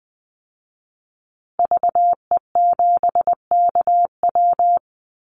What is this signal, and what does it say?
Morse code, heard as a keyed tone, about 20 wpm, 700 Hz. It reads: VE7KW
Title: Call Signs Repeated 2x - 20wpm